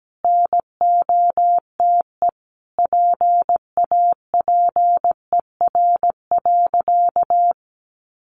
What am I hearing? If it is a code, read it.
Morse code, heard as a keyed tone, about 17 wpm, 700 Hz. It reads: NOTE PAPER.
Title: Sentences from Top 1000 Words 17wpm